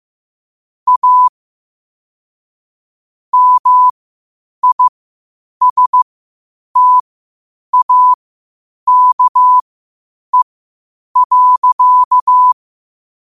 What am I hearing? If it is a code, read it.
Morse code, heard as a keyed tone, about 15 wpm, spaced as for 8 wpm, 1000 Hz. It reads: A MISTAKE.
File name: qotdfiles / 8wpm